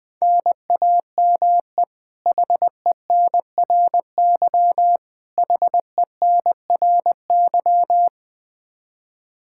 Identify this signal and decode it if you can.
Morse code, heard as a keyed tone, about 20 wpm, 700 Hz. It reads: NAME HENRY HENRY
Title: Practice QSO - Exchange 1 - Beginning - 20wpm